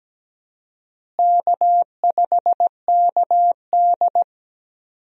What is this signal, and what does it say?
Morse code, heard as a keyed tone, about 17 wpm, 700 Hz. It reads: K5KD
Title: USA Call Signs - 4 Characters - Repeated 2x - 17wpm